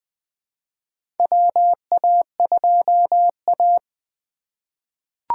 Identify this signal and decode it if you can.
Morse code, heard as a keyed tone, about 20 wpm, 700 Hz. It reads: WA2A
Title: USA Call Signs - 4 Characters - 2x Character Spacing - 20wpm